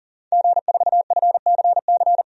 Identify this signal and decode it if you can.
Morse code, heard as a keyed tone, about 39 wpm, 700 Hz. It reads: G4FCC
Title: DXCC-100 Call Signs - 5 Characters - 40wpm